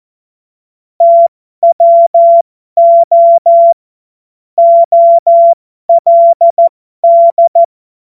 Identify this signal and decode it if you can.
Morse code, heard as a keyed tone, about 14 wpm, spaced as for 12 wpm, 680 Hz. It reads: TWO OLD
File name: Morsekurs / Witze